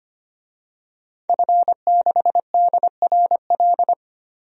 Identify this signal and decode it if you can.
Morse code, heard as a keyed tone, about 25 wpm, 700 Hz. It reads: F6DRL
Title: DXCC-100 Call Signs - 5 Characters - Repeated 3x - 25wpm